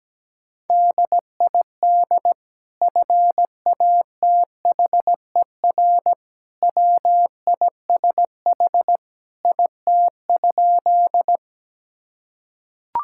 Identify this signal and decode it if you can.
Morse code, heard as a keyed tone, about 17 wpm, 700 Hz. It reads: DID FATHER WISH IT?
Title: Sentences from Top 1600 Words 17wpm